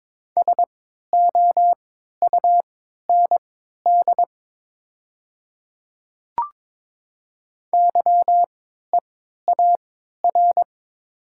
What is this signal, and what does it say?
Morse code, heard as a keyed tone, about 22 wpm, spaced as for 12 wpm, 700 Hz. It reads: SOUND YEAR
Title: 3x Character Spacing - Top 200 Words - 22wpm